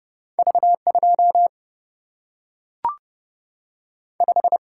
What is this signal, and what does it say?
Morse code, heard as a keyed tone, about 30 wpm, 700 Hz. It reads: V 2 5
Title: Single-Letter Single-Number 30wpm